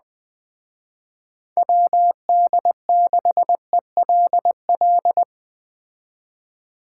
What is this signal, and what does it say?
Morse code, heard as a keyed tone, about 20 wpm, 700 Hz. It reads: WD6ELL
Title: Speed Racing - USA 2x3 Call Signs 17wpm